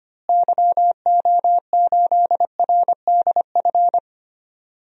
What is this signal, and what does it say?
Morse code, heard as a keyed tone, about 25 wpm, 700 Hz. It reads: YO8RDF
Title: DXCC-100 Call Signs - 6 Characters - Repeated 2x - 25wpm